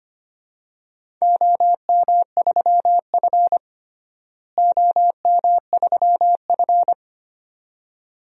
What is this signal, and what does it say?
Morse code, heard as a keyed tone, about 25 wpm, 700 Hz. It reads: OM3F OM3F
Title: DXCC-100 Call Signs - 4 Characters - Repeated 2x - 25wpm